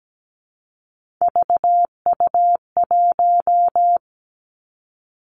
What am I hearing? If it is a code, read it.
Morse code, heard as a keyed tone, about 17 wpm, 700 Hz. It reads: VU1
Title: Three Challenging Letter-Number Combinations 17wpm